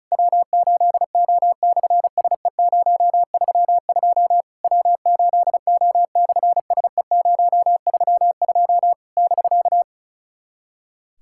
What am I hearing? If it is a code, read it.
Morse code, heard as a keyed tone, about 36 wpm, 700 Hz. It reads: W8O/SE032 W8O/SE032 <BK>
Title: SOTA QSO Practice 35wpm